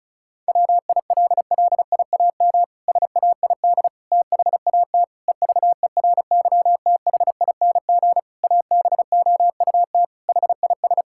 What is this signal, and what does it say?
Morse code, heard as a keyed tone, about 34 wpm, 700 Hz. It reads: WILLIAM SAID THAT EVERYTHING ABOUT HIS
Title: Pangram Challenge v2 - 35wpm